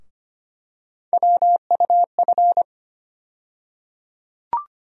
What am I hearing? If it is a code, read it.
Morse code, heard as a keyed tone, about 25 wpm, 700 Hz. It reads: WUF
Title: USA Call Sign Postfixes - 3 Letters 25wpm